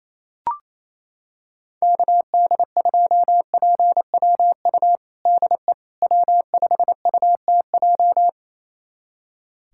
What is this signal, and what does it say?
Morse code, heard as a keyed tone, about 28 wpm, 700 Hz. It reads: KD2PWU DE W5UTJ
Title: USA Callsign DE Callsign 28wpm